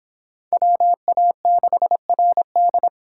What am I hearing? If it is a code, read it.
Morse code, heard as a keyed tone, about 26 wpm, 700 Hz. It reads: WA6RD